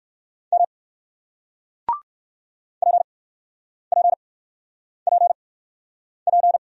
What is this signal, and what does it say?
Morse code, heard as a keyed tone, about 60 wpm, spaced as for 9 wpm, 700 Hz. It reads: M PPPP